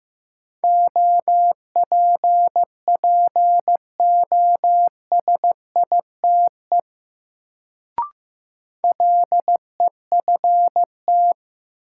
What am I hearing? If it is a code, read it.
Morse code, heard as a keyed tone, about 16 wpm, 700 Hz. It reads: OPPOSITE LEFT
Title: Sets of 5 Words Encore - New Word Drill - 15wpm